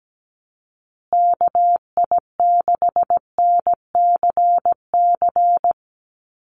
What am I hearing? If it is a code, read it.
Morse code, heard as a keyed tone, about 17 wpm, 700 Hz. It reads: KI6NCC
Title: USA Call Signs Repeated 3x 17wpm